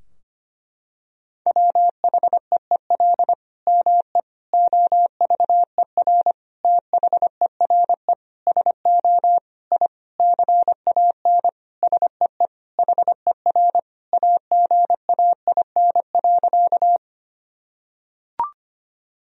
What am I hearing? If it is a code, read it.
Morse code, heard as a keyed tone, about 25 wpm, 700 Hz. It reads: WHEEL ME OVER THERE SO I CAN SEE HER AGAIN.
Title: Sentences from Top 1000 Words - Review Part 3 - 25wpm